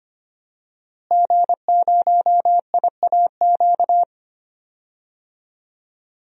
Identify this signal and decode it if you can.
Morse code, heard as a keyed tone, about 25 wpm, 700 Hz. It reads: G0IAQ
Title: DXCC-100 Call Signs - 5 Characters - Repeated 3x - 25wpm